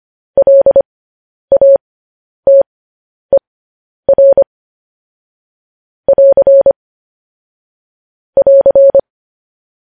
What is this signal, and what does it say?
Morse code, heard as a keyed tone, about 25 wpm, spaced as for 10 wpm, 550 Hz. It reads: LATER <AR> <AR>
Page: News Headlines in Morse Code at 10 WPM